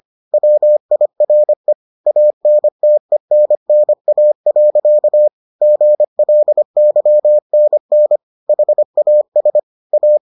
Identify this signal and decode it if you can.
Morse code, heard as a keyed tone, about 25 wpm, 600 Hz. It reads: WIRE ANTENNA. GLYNN HAS A